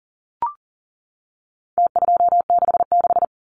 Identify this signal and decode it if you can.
Morse code, heard as a keyed tone, about 40 wpm, 700 Hz. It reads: T266
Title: Numbers - Contest Style - 4-Digit 40wpm